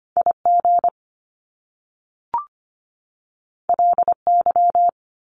Morse code, heard as a keyed tone, about 25 wpm, 700 Hz. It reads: IG LY
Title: Top Two Letter Combinations in Words 25wpm